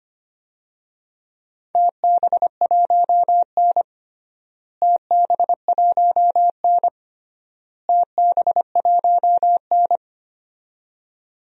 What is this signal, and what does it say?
Morse code, heard as a keyed tone, about 25 wpm, 700 Hz. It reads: TB1N TB1N TB1N
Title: DXCC-100 Call Signs - 4 Characters - Repeated 3x - 25wpm